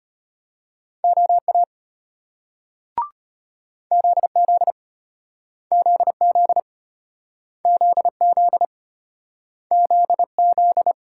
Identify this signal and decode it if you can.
Morse code, heard as a keyed tone, about 34 wpm, 700 Hz. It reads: OA ZZ ZZ ZZ ZZ